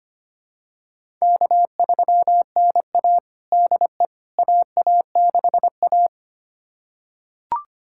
Morse code, heard as a keyed tone, about 25 wpm, 700 Hz. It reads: K3NA DE AA6A
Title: USA Callsign DE Callsign - 4 Characters - 25wpm